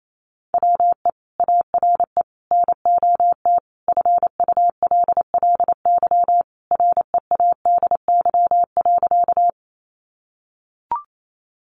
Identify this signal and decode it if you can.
Morse code, heard as a keyed tone, about 28 wpm, 700 Hz. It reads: WE ARE NOT FULLY READY.
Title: Sentences from Top 1300 Words 28wpm